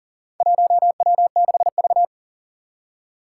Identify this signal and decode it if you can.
Morse code, heard as a keyed tone, about 40 wpm, 700 Hz. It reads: 1W6V
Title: Four Challenging Letter-Number Combinations 40wpm